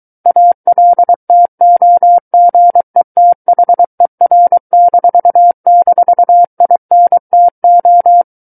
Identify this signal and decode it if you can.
Morse code, heard as a keyed tone, about 23 wpm, 700 Hz. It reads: ALTOGETHER--INTO
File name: Book One / 14-Chapter 14 - 23 WPM